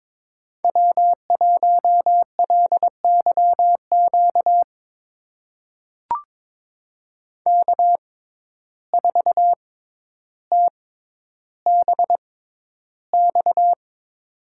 Morse code, heard as a keyed tone, about 23 wpm, spaced as for 11 wpm, 700 Hz. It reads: W1LYQ K4TBX
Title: USA Call Signs - 5 Characters - 6x Character Spacing - 22wpm